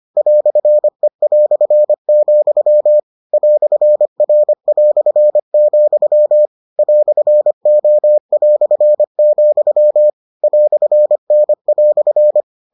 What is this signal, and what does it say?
Morse code, heard as a keyed tone, about 25 wpm, 600 Hz. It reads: "E", "R", "O", "N"